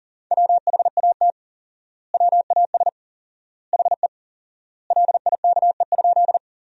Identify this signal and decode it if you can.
Morse code, heard as a keyed tone, about 37 wpm, 700 Hz. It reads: WHAT WAS HE LIKE?
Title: 4x Word-Spacing - Sentences from Top 100 Words - 40wpm